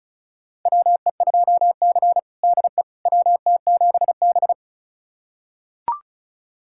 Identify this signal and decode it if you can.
Morse code, heard as a keyed tone, about 34 wpm, 700 Hz. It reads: WE2C DE WT7B
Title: USA Callsign DE Callsign - 4 Characters - 35wpm